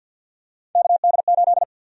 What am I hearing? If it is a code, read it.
Morse code, heard as a keyed tone, about 52 wpm, 700 Hz. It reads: KD8
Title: USA Call Sign Prefixes - Repeated 2x - 50wpm